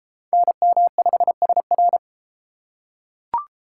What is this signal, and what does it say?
Morse code, heard as a keyed tone, about 31 wpm, 700 Hz. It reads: NM5SR